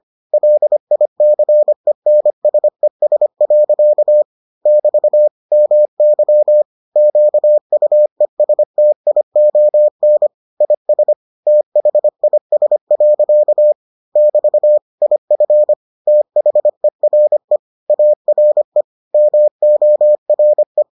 Morse code, heard as a keyed tone, about 25 wpm, 600 Hz. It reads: LICENSES. <BT> MY QUESTION IS THIS. <BT> IF THERE ARE MORE